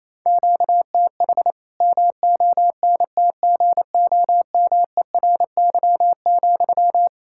Morse code, heard as a keyed tone, about 28 wpm, 700 Hz. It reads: QTH MONTGOMERY,
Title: K1USN SST QSO Practice 28wpm